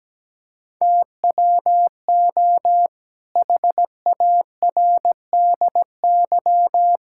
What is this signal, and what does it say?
Morse code, heard as a keyed tone, about 17 wpm, 700 Hz. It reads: TWO HARDY
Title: Pangram Challenge v2 - 17wpm